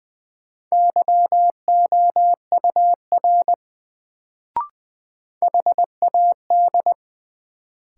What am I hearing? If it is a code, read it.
Morse code, heard as a keyed tone, about 20 wpm, 700 Hz. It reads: YOUR HAD
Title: Sets of 2 Words - New Word Drill - 20wpm